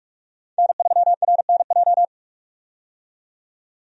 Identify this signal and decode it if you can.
Morse code, heard as a keyed tone, about 46 wpm, 700 Hz. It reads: N3RNJ
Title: USA Call Signs - 5 Characters - 45wpm